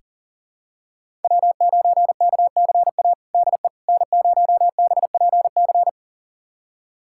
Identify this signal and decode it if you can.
Morse code, heard as a keyed tone, about 40 wpm, 700 Hz. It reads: W9KCA DE N0BPC
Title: USA Callsign DE Callsign - 5 Characters - 40wpm